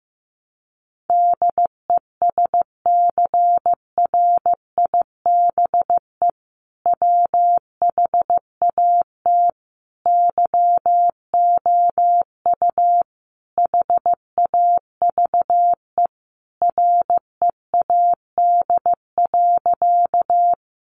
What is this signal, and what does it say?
Morse code, heard as a keyed tone, about 14 wpm, 700 Hz. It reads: DESCRIBE WHAT YOU HAVE READ.
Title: Sentences from Top 1000 Words 15wpm